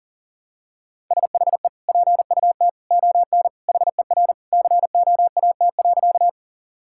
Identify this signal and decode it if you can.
Morse code, heard as a keyed tone, about 40 wpm, 700 Hz. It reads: SHE PUT ON HER COAT.
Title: Sentences from Top 1600 Words 40wpm